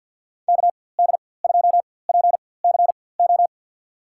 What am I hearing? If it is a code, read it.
Morse code, heard as a keyed tone, about 51 wpm, spaced as for 23 wpm, 700 Hz. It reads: KD2PCO
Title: USA Call Signs - 6 Characters - 4x Character Spacing - 50wpm